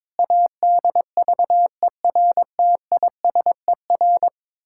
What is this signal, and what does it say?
Morse code, heard as a keyed tone, about 22 wpm, 700 Hz. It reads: ADVERTISER
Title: Pangram Challenge v2 - 22wpm